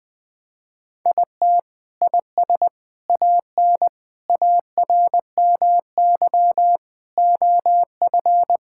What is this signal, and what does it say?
Morse code, heard as a keyed tone, about 21 wpm, 700 Hz. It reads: IT IS AN ARMY OF
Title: Sentences from Top 1000 Words - Review Part 5 - 20wpm